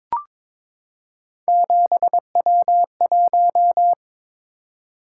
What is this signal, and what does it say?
Morse code, heard as a keyed tone, about 22 wpm, 700 Hz. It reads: 7W1
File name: practice / 22wpm / Three Letter-Number Combinations - 22wpm